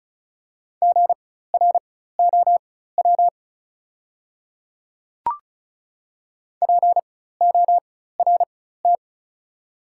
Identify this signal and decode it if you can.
Morse code, heard as a keyed tone, about 36 wpm, spaced as for 16 wpm, 700 Hz. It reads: GROW PORT
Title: 4x Character Spacing - Top 300 Words - 35wpm